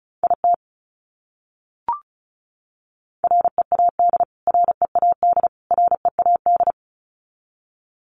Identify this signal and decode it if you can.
Morse code, heard as a keyed tone, about 34 wpm, 700 Hz. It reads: IT READ READ READ
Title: Top 200 Words Repeated 35wpm